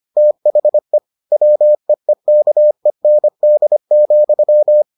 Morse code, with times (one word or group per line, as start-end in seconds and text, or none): 0.2-1.0 THE
1.3-4.8 WEEKEND,